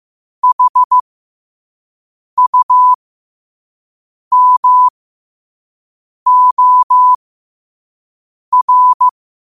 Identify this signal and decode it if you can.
Morse code, heard as a keyed tone, about 15 wpm, spaced as for 5 wpm, 1000 Hz. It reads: HUMOR
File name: qotdfiles / 5wpm